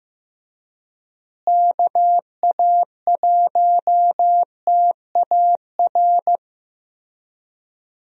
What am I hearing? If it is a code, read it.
Morse code, heard as a keyed tone, about 16 wpm, 700 Hz. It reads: KA1TAR
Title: USA Call Signs - 6 Characters - Repeated 2x - 15wpm